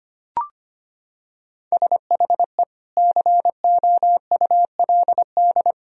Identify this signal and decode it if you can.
Morse code, heard as a keyed tone, about 25 wpm, 700 Hz. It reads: SHE COULD
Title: Sentences from Top 1000 Words - Review Part 3 - 25wpm